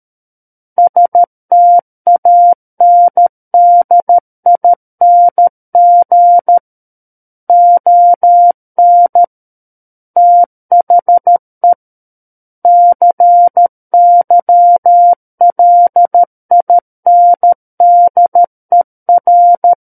Morse code, heard as a keyed tone, about 13 wpm, 700 Hz. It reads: STANDING ON THE CYLINDER
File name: Book One / 04-Chapter 4 - 13 WPM